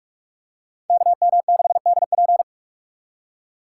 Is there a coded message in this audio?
Yes: Morse code from 0.9 to 2.4 s, heard as a keyed tone, about 41 wpm, 700 Hz. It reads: KM6DP